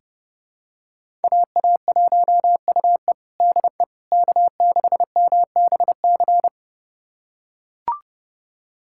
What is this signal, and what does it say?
Morse code, heard as a keyed tone, about 30 wpm, 700 Hz. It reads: AA1UE DE K6MBC
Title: USA Callsign DE Callsign - 5 Characters - 30wpm